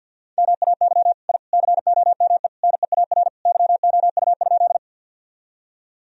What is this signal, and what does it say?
Morse code, heard as a keyed tone, about 48 wpm, 700 Hz. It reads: MAY I COME NEAR YOU?
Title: Sentences from Top 1000 Words - Review Part 1 - 50wpm